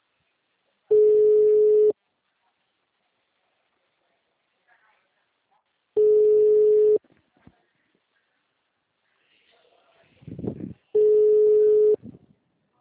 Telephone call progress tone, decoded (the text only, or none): ringback tone